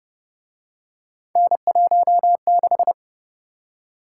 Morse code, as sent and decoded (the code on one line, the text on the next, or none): -. .---- -....
N16